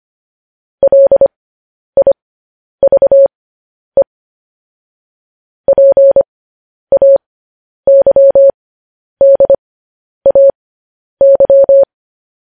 Morse code, heard as a keyed tone, about 25 wpm, spaced as for 10 wpm, 550 Hz. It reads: LIVE PAYDAY
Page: news-headlines-in-morse-code-at-10-wpm